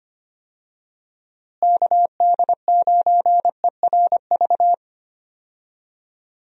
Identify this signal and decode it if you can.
Morse code, heard as a keyed tone, about 25 wpm, 700 Hz. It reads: KD9ERV
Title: USA Call Signs - 6 Characters - Repeated 3x - 25wpm